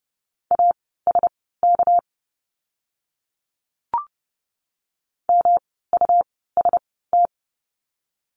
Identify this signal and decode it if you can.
Morse code, heard as a keyed tone, about 30 wpm, spaced as for 17 wpm, 700 Hz. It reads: ASK MUST
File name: course / 30:10wpm / 237 Lesson 38f - Letter q - Words 30wpm